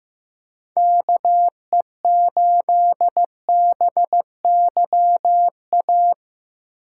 Morse code, heard as a keyed tone, about 15 wpm, 700 Hz. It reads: KE8BYA
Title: USA Call Signs - 8x Character Spacing - 15wpm